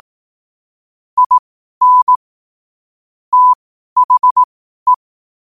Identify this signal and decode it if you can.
Morse code, heard as a keyed tone, about 18 wpm, spaced as for 12 wpm, 1000 Hz. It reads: IN THE
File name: 12wpm